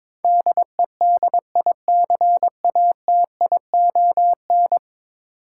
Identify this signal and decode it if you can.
Morse code, heard as a keyed tone, about 22 wpm, 700 Hz. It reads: DEDICATION
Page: morse-code-ninja-practice-22wpm